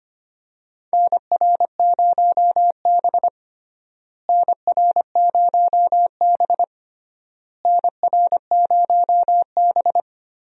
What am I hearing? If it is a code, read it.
Morse code, heard as a keyed tone, about 25 wpm, 700 Hz. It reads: NR0B NR0B NR0B
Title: USA Call Signs - 4 Characters - Repeated 3x - 25wpm